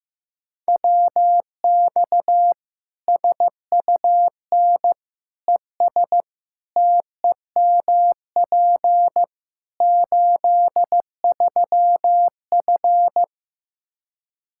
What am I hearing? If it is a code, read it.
Morse code, heard as a keyed tone, about 15 wpm, 700 Hz. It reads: WX SUN ES TEMP 83F
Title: K1USN SST QSO Practice 15wpm